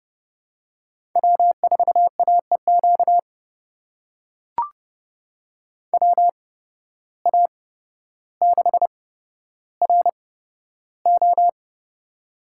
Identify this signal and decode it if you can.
Morse code, heard as a keyed tone, about 31 wpm, spaced as for 13 wpm, 700 Hz. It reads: W4AEQ WA6RO
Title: USA Call Signs - 5 Characters - 8x Character Spacing - 30wpm